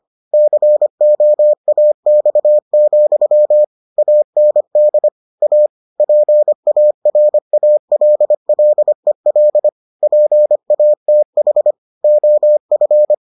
Morse code, heard as a keyed tone, about 25 wpm, 600 Hz. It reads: COAX, AND A PARALLEL PATH OF